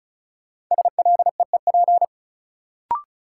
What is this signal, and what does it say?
Morse code, heard as a keyed tone, about 36 wpm, 700 Hz. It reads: SLEEP
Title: Sentences from Top 500 Words - New Word Drill - 35wpm